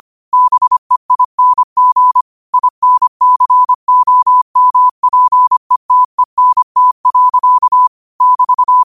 Morse code, heard as a keyed tone, about 25 wpm, 1000 Hz. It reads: BEING INCOMPETENT. <BT>